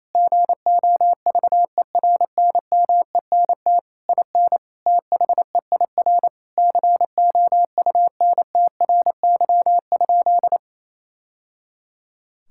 Morse code, heard as a keyed tone, about 28 wpm, 700 Hz. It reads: GOVERNMENT IN THEIR COUNTRY?
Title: Sentences from Top 1000 Words - Review Part 5 - 28wpm